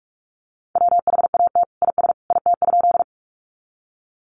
Morse code, heard as a keyed tone, about 44 wpm, 700 Hz. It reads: WHAT IS IT?
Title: Sentences from Top 100 Words 45wpm